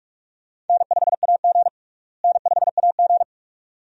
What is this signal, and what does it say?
Morse code, heard as a keyed tone, about 44 wpm, 700 Hz. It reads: N5AG N5AG
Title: USA Call Signs - 4 Characters - Repeated 2x - 45wpm